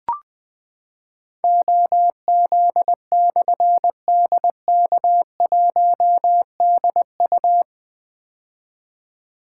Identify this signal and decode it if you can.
Morse code, heard as a keyed tone, about 20 wpm, 700 Hz. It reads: OZ/DK1DU
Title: Difficult Call Signs 20wpm